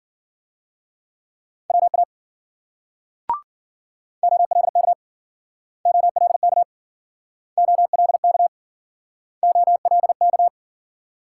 Morse code, heard as a keyed tone, about 53 wpm, 700 Hz. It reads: WA OLK OLK OLK OLK